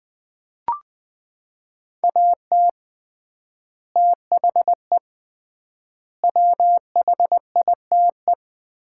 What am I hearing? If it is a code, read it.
Morse code, heard as a keyed tone, about 21 wpm, 700 Hz. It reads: AT THE WHITE
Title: 3x Word-Spacing - Sets of 4 Words - 20wpm